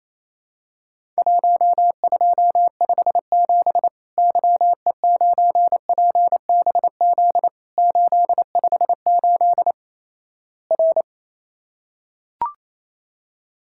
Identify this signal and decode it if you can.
Morse code, heard as a keyed tone, about 27 wpm, 700 Hz. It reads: 1257 YE9PBZ 858 R
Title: Worked All Europe DX Contest - QTC Exchange - 28wpm